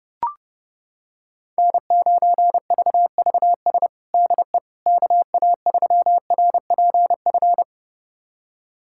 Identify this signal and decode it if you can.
Morse code, heard as a keyed tone, about 30 wpm, 700 Hz. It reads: N9VVS DE KA3RPF
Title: USA Callsign DE Callsign 30wpm